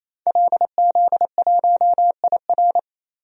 Morse code, heard as a keyed tone, about 28 wpm, 700 Hz. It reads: LZ1IR
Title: DXCC-100 Call Signs - 5 Characters - Repeated 3x - 28wpm